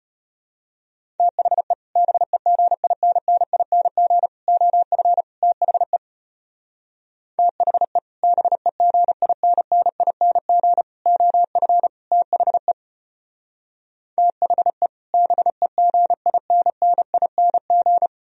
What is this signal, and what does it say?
Morse code, heard as a keyed tone, about 31 wpm, 690 Hz. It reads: THE BEGINNING OF THE THE BEGINNING OF THE THE BEGINNING
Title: Speed-Racing - Sets of 4 Words 25wpm